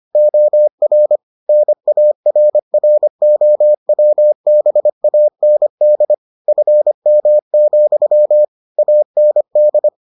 Morse code, heard as a keyed tone, about 25 wpm, 600 Hz. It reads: OR NARROWBAND FM, AND